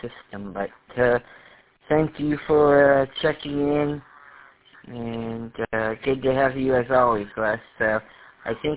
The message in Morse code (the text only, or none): none